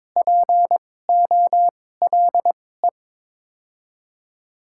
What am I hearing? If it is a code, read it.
Morse code, heard as a keyed tone, about 23 wpm, spaced as for 16 wpm, 700 Hz. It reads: POLE
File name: Top 90 QSO Elements - 2x Character Spacing - 22wpm